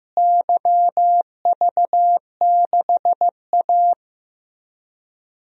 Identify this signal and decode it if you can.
Morse code, heard as a keyed tone, about 15 wpm, 700 Hz. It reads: YV6A
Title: DXCC-100 Call Signs - 4 Characters - Repeated 2x - 15wpm